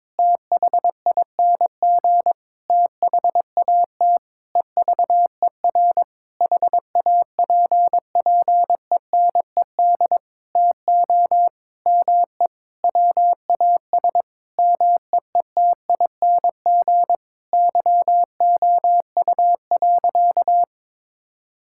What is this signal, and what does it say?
Morse code, heard as a keyed tone, about 22 wpm, 700 Hz. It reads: THING THAT EVER HAPPENED TO ME WAS MEETING YOU.
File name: Sentences with 7-Word Phrases - 22wpm